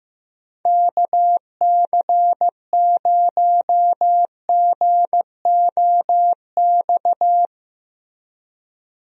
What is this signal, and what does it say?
Morse code, heard as a keyed tone, about 15 wpm, 700 Hz. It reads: KC0GOX
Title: USA Call Signs - 6 Characters - Repeated 3x - 15wpm